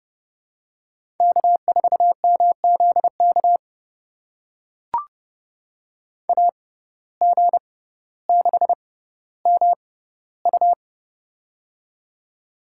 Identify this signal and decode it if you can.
Morse code, heard as a keyed tone, about 30 wpm, spaced as for 15 wpm, 700 Hz. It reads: K4MZK AG6MU
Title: USA Call Signs - 5 Characters - 6x Character Spacing - 30wpm